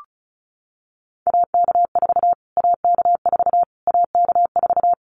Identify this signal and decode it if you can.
Morse code, heard as a keyed tone, about 34 wpm, 700 Hz. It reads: AK4 AK4 AK4
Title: Callsign Prefixes for Top 25 DXCC - Repeated 3x - 35wpm